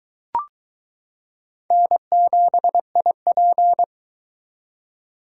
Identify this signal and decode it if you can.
Morse code, heard as a keyed tone, about 22 wpm, 700 Hz. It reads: N7IP